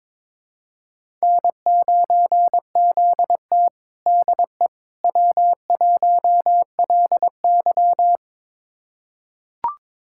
Morse code, heard as a keyed tone, about 22 wpm, 700 Hz. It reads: N9ZT DE W1LY
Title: USA Callsign DE Callsign - 4 Characters - 22wpm